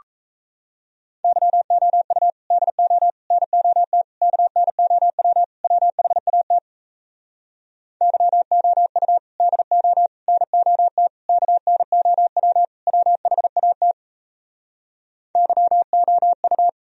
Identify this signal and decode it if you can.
Morse code, heard as a keyed tone, about 38 wpm, 700 Hz. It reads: YOU DO NOT KNOW WHAT YOU DO NOT KNOW WHAT YOU